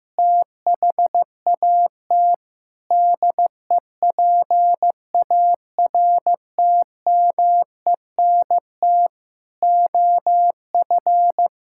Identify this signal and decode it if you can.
Morse code, heard as a keyed tone, about 15 wpm, 700 Hz. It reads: THAT DEPARTMENT OF
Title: Sentences from Top 1000 Words - Review Part 5 - 15wpm